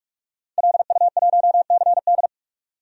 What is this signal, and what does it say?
Morse code, heard as a keyed tone, about 46 wpm, 700 Hz. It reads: LU1CD
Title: DXCC-100 Call Signs - 5 Characters - 45wpm